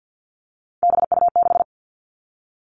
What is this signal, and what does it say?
Morse code, heard as a keyed tone, about 50 wpm, 700 Hz. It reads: BU6
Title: Three Challenging Letter-Number Combinations 50wpm